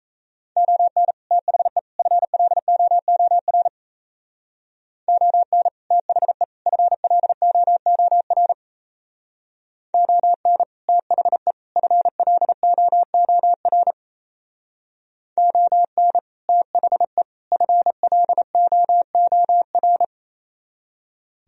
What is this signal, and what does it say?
Morse code, heard as a keyed tone, about 33 wpm, 700 Hz. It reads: ON THE FLOOR ON THE FLOOR ON THE FLOOR ON THE FLOOR
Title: Speed-Racing - Sets of 3 Words 28wpm